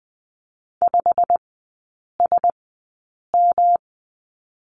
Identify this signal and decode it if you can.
Morse code, heard as a keyed tone, about 20 wpm, spaced as for 8 wpm, 700 Hz. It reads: 5SM
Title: 107 Lesson 18e - Letter m - Three Characters 20wpm